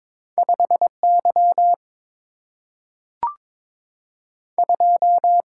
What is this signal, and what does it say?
Morse code, heard as a keyed tone, about 22 wpm, 700 Hz. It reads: 5Y 2
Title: Single-Letter Single-Number 22wpm